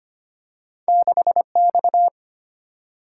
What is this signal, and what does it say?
Morse code, heard as a keyed tone, about 25 wpm, 690 Hz. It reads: 6X
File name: Two Letter-Number Combinations - 25wpm